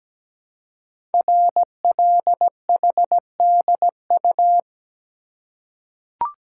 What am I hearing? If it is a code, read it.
Morse code, heard as a keyed tone, about 17 wpm, 700 Hz. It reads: RLHDU
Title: Code Groups - 5 Letters - 17wpm